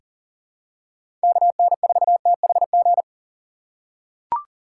Morse code, heard as a keyed tone, about 40 wpm, 700 Hz. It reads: KN4THG
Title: USA Call Signs - 6 Characters - 6x Character Spacing - 40wpm